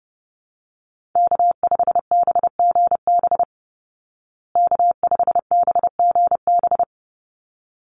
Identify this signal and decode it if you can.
Morse code, heard as a keyed tone, about 30 wpm, 700 Hz. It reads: K5BGB K5BGB
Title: Call Signs Repeated 3x - 30wpm